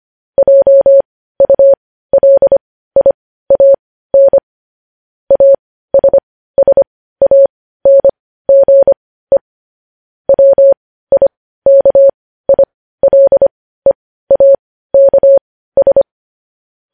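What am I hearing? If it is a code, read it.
Morse code, heard as a keyed tone, about 25 wpm, spaced as for 15 wpm, 550 Hz. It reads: JULIAN ASSANGE WIKILEAKS
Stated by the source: Morse news headlines sent at 15 words per minute.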